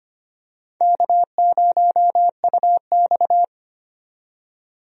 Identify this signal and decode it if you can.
Morse code, heard as a keyed tone, about 25 wpm, 700 Hz. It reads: K0UX
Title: USA Call Signs - 4 Characters - 25wpm